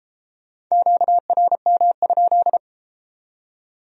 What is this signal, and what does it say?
Morse code, heard as a keyed tone, about 29 wpm, 700 Hz. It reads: QRM?